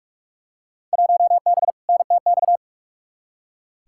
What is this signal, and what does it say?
Morse code, heard as a keyed tone, about 46 wpm, 700 Hz. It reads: 1B NTX
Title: Field Day Exchanges - 45wpm